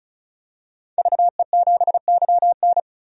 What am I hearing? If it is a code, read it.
Morse code, heard as a keyed tone, about 36 wpm, 700 Hz. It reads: VE7YN
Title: DXCC-100 Call Signs - 5 Characters - 35wpm